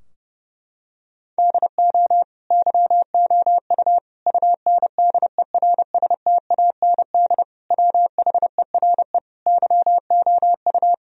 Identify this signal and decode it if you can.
Morse code, heard as a keyed tone, about 30 wpm, 700 Hz. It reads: DO YOU UNDERSTAND WHERE YOU
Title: Sentences from Top 1000 Words - Review Part 4 - 30wpm